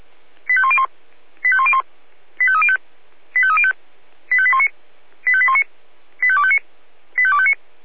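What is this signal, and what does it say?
Five-tone selective call (ZVEI), heard as quick runs of five tones, tones of about 70 ms. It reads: 85101 85101 85305 85305 85819 85819 85259 85259
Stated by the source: ZVEI five-tone selective call received on SDR.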